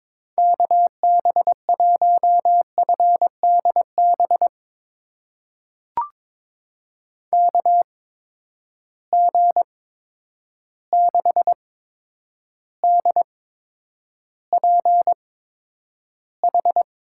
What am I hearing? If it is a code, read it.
Morse code, heard as a keyed tone, about 22 wpm, spaced as for 9 wpm, 700 Hz. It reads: KB1FDB KG6DPH
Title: USA Call Signs - 6 Characters - 8x Character Spacing - 22wpm